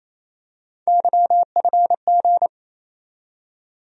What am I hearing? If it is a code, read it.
Morse code, heard as a keyed tone, about 28 wpm, 700 Hz. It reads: YFG
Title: USA Call Sign Postfixes - 3 Letters 28wpm